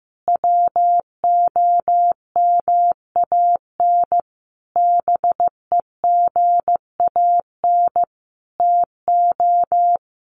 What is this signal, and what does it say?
Morse code, heard as a keyed tone, about 14 wpm, 700 Hz. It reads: WOMAN BEGAN TO
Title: Sentences from Top 1100 Words 15wpm